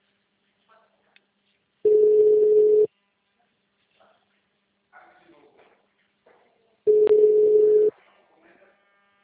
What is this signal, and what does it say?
Ringback tone (Europe, 425 Hz).
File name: out- -2031- - - .1404